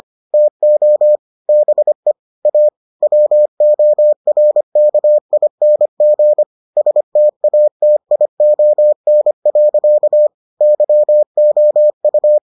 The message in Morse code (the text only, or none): TO BE A WORKING STATION. YOU